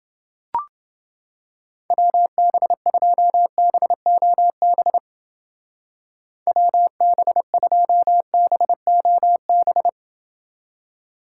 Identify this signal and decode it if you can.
Morse code, heard as a keyed tone, about 29 wpm, 700 Hz. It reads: WB2BOB WB2BOB